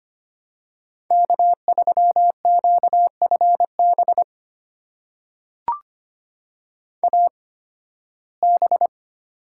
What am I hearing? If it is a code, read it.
Morse code, heard as a keyed tone, about 25 wpm, spaced as for 16 wpm, 700 Hz. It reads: K3QFB AB
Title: USA Call Signs - 5 Characters - 8x Character Spacing - 25wpm